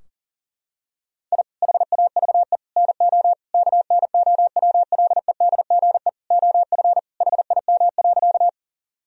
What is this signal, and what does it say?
Morse code, heard as a keyed tone, about 40 wpm, 700 Hz. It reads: I HAVE NO KNOWLEDGE OF HIM.
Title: Sentences from Top 1000 Words - Review Part 3 - 40wpm